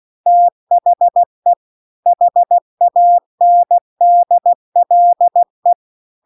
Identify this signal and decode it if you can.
Morse code, heard as a keyed tone, about 16 wpm, 700 Hz. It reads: THE HANDLE